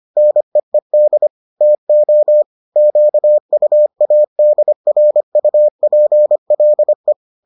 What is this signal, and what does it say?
Morse code, heard as a keyed tone, about 25 wpm, 600 Hz. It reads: NEED TO QUADRUPLE